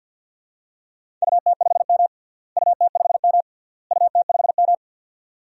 Morse code, heard as a keyed tone, about 47 wpm, 700 Hz. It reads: UT5M UT5M UT5M
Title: DXCC-100 Call Signs - 4 Characters - Repeated 3x - 50wpm